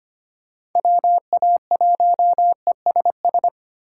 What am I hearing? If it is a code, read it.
Morse code, heard as a keyed tone, about 25 wpm, 700 Hz. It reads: WA1ESS
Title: USA Call Signs - 2x Character Spacing - 25wpm